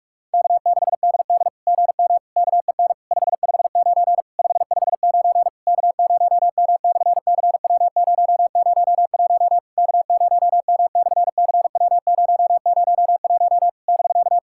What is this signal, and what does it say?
Morse code, heard as a keyed tone, about 48 wpm, 700 Hz. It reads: K6DD GM KEN 559 559 K0M/CW001 K0M/CW001 <BK>
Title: SOTA QSO Practice 45wpm